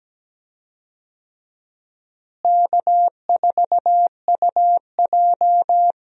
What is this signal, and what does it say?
Morse code, heard as a keyed tone, about 17 wpm, 700 Hz. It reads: K4UJ
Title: USA Call Signs - 4 Characters - Repeated 3x - 17wpm